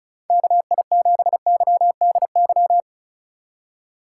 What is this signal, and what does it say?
Morse code, heard as a keyed tone, about 36 wpm, 700 Hz. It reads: KI7YDY
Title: USA Call Signs - 6 Characters - 3x Character Spacing - 35wpm